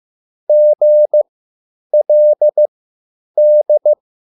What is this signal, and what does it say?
Morse code, heard as a keyed tone, about 15 wpm, spaced as for 9 wpm, 600 Hz. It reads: GLD
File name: QSL / CW lesson